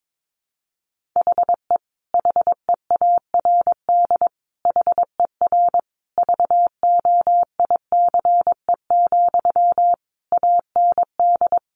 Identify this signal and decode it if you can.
Morse code, heard as a keyed tone, about 22 wpm, 700 Hz. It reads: HE HEARD HER VOICE, AND
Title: Sentences from Top 1100 Words 22wpm